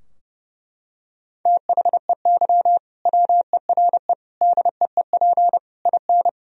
Morse code, heard as a keyed tone, about 32 wpm, 700 Hz. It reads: THEY WERE DEEP IN
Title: Sentences from Top 1000 Words - Review Part 3 - 30wpm